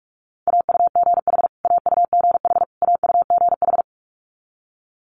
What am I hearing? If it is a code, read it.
Morse code, heard as a keyed tone, about 46 wpm, 700 Hz. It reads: AUGH AUGH AUGH
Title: Multi-letter Phonemes Repeated 3x 45wpm